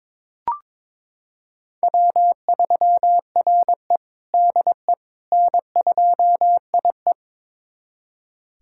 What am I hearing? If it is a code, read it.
Morse code, heard as a keyed tone, about 22 wpm, 700 Hz. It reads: W3RE DE N2IE
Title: USA Callsign DE Callsign - 4 Characters - 22wpm